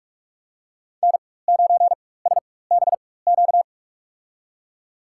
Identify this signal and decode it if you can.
Morse code, heard as a keyed tone, about 44 wpm, spaced as for 21 wpm, 690 Hz. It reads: N9SBQ
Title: USA Call Signs - 4x Character Spacing - 45wpm